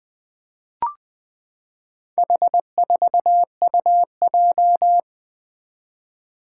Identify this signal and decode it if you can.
Morse code, heard as a keyed tone, about 20 wpm, 700 Hz. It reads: H4UJ
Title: Four Challenging Letter-Number Combinations 20wpm